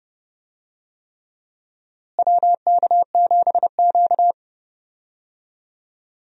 Morse code, heard as a keyed tone, about 31 wpm, 700 Hz. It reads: WK7Q
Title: USA Call Signs - 4 Characters - Repeated 3x - 30wpm